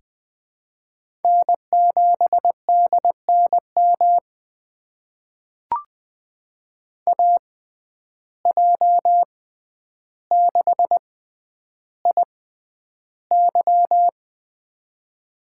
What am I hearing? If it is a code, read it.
Morse code, heard as a keyed tone, about 21 wpm, spaced as for 10 wpm, 700 Hz. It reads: N7DNM AJ6IY
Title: USA Call Signs - 5 Characters - 6x Character Spacing - 20wpm